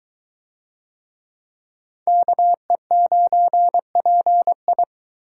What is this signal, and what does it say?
Morse code, heard as a keyed tone, about 22 wpm, 700 Hz. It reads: KE9PI